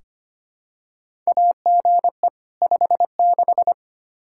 Morse code, heard as a keyed tone, about 25 wpm, 700 Hz. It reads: AGE 56
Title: K1USN SST QSO Practice 25wpm